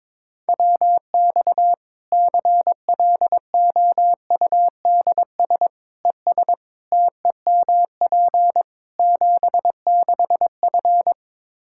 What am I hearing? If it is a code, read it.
Morse code, heard as a keyed tone, about 21 wpm, 700 Hz. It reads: WX CLOUDS ES TEMP 76F
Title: K1USN SST QSO Practice 22wpm